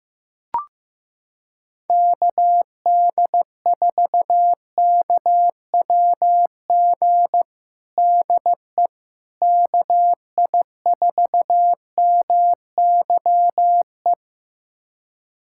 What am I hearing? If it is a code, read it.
Morse code, heard as a keyed tone, about 15 wpm, 700 Hz. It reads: KD4KWG DE KI4MYE
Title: USA Callsign DE Callsign 15wpm